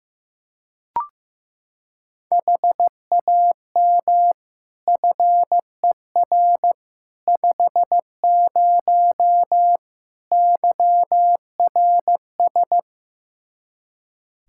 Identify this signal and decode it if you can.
Morse code, heard as a keyed tone, about 15 wpm, 700 Hz. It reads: HAM FER 50 YRS
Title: Practice QSO - Exchange 3 - More Info - 15wpm